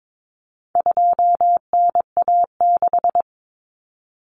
Morse code, heard as a keyed tone, about 22 wpm, 700 Hz. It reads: 2NA6
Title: Common Cut Numbers - 4-Digit 22wpm